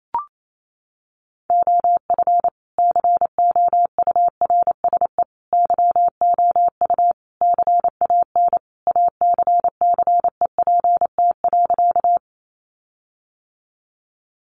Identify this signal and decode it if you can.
Morse code, heard as a keyed tone, about 28 wpm, 700 Hz. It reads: OF COURSE YOU CAN ACCEPT.
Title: Sentences from Top 1300 Words 28wpm